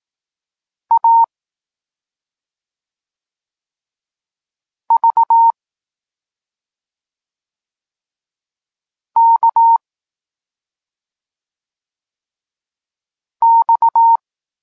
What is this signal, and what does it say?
Morse code, heard as a keyed tone, about 18 wpm, 920 Hz. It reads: A V K X
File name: Tekst-34 4